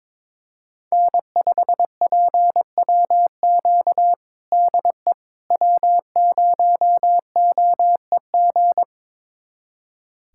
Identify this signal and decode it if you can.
Morse code, heard as a keyed tone, about 22 wpm, 700 Hz. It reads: N5PWQ DE W0OEG
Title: USA Callsign DE Callsign - 5 Characters - 22wpm